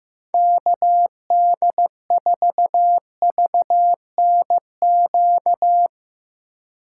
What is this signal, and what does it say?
Morse code, heard as a keyed tone, about 15 wpm, 700 Hz. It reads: KD4VNQ
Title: USA Call Signs - 6 Characters - 2x Character Spacing - 15wpm